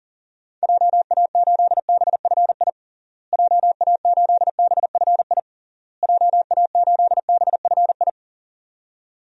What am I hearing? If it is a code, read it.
Morse code, heard as a keyed tone, about 40 wpm, 700 Hz. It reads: JA8BFI JA8BFI JA8BFI
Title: DXCC-100 Call Signs - 6 Characters - Repeated 3x - 40wpm